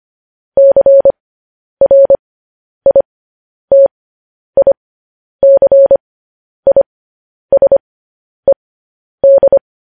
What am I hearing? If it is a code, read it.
Morse code, heard as a keyed tone, about 25 wpm, spaced as for 10 wpm, 550 Hz. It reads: CRITICISED
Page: News Headlines in Morse Code at 10 WPM